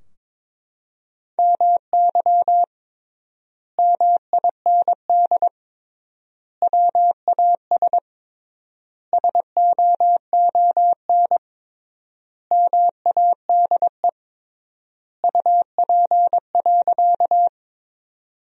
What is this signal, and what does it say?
Morse code, heard as a keyed tone, about 22 wpm, 700 Hz. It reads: MY MIND WAS SOON MADE UP.
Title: 3x Word-Spacing - Sentences from Top 500 Words - 22wpm